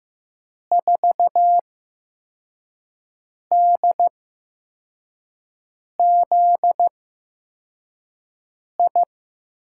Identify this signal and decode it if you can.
Morse code, heard as a keyed tone, about 15 wpm, spaced as for 4 wpm, 700 Hz. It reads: 4DZI
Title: USA Call Signs - 6 Characters - 8x Character Spacing - 15wpm